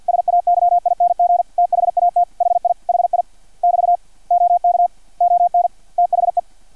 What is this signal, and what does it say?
Morse code, heard as a keyed tone, about 50 wpm, 700 Hz, with radio noise. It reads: SAYING THAT HI HI <BT> OK ON THE
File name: archived / CRSnet